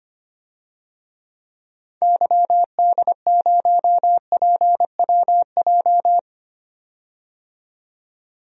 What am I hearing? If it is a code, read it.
Morse code, heard as a keyed tone, about 25 wpm, 700 Hz. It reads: YD0PWJ
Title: DXCC-100 Call Signs - 6 Characters - 25wpm